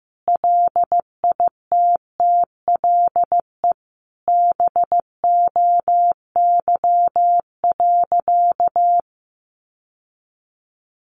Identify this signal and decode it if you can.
Morse code, heard as a keyed tone, about 15 wpm, 700 Hz. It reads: LITTLE BOY.
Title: Sentences from Top 1000 Words 15wpm